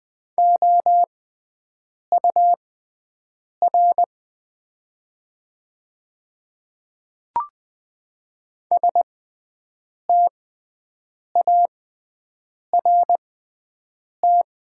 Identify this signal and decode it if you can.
Morse code, heard as a keyed tone, about 20 wpm, spaced as for 7 wpm, 700 Hz. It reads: OUR START
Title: 6x Character Spacing - Top 300 Words - 20wpm